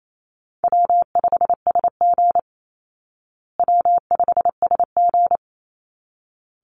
Morse code, heard as a keyed tone, about 28 wpm, 700 Hz. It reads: W5SG W5SG
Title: Easy Call Signs Repeated 2x - 28wpm